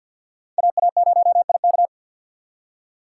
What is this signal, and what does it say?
Morse code, heard as a keyed tone, about 61 wpm, 700 Hz. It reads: AA0IK
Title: Speed Racing - USA 2x2 Call Signs 50wpm